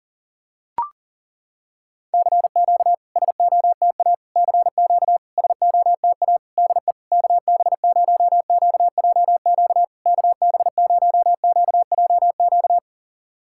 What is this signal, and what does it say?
Morse code, heard as a keyed tone, about 40 wpm, 700 Hz. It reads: CQ SOTA CQ SOTA DE KB0QJQ KB0QJQ
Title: SOTA QSO Practice 40wpm